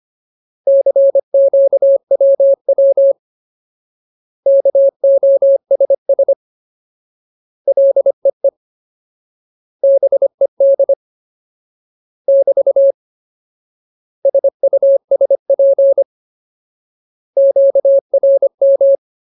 Morse code, heard as a keyed tone, about 25 wpm, 550 Hz. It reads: CQWW KOSS LEE BED <BT> SUSP QRM
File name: Ultimate Word List / 25WPM Ultimate